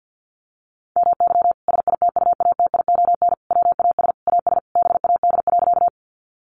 Morse code, heard as a keyed tone, about 50 wpm, 700 Hz. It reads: MY SITUATION WAS AS BAD.
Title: Sentences from Top 1200 Words 50wpm